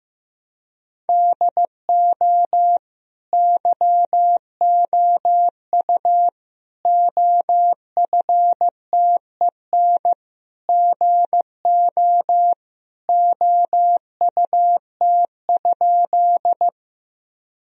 Morse code, heard as a keyed tone, about 16 wpm, 700 Hz. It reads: DO YOU OFTEN GO OUT?
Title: Sentences from Top 1000 Words - Review Part 1 - 15wpm